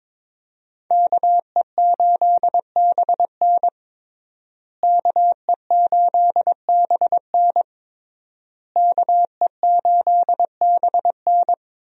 Morse code, heard as a keyed tone, about 22 wpm, 700 Hz. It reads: KE8BN KE8BN KE8BN
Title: USA Call Signs - 5 Characters - Repeated 3x - 22wpm